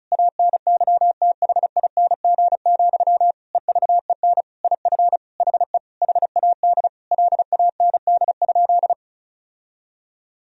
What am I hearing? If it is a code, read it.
Morse code, heard as a keyed tone, about 34 wpm, 700 Hz. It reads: ANYTHING, EVEN IF HE HAD LAND?
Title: Sentences from Top 1000 Words - Review Part 1 - 35wpm